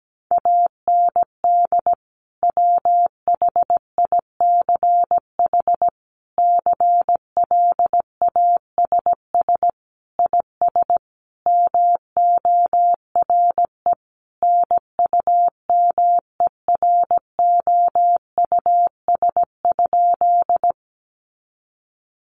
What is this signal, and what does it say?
Morse code, heard as a keyed tone, about 17 wpm, 700 Hz. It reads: AND WHICH CLASS IS MORE NUMEROUS?
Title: Sentences from Top 1400 Words 17wpm